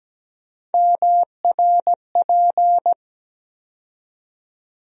Morse code, heard as a keyed tone, about 17 wpm, 700 Hz. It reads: MRP
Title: USA Call Sign Postfixes - 3 Letters 17wpm